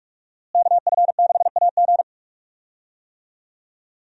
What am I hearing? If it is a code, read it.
Morse code, heard as a keyed tone, about 42 wpm, 700 Hz. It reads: KF6AG
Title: USA Call Signs - 5 Characters - Repeated 2x - 45wpm